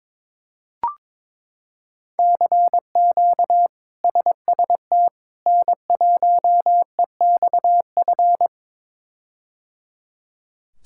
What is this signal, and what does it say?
Morse code, heard as a keyed tone, about 22 wpm, 700 Hz. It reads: CQ SST N1EXF
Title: K1USN SST QSO Practice - 22wpm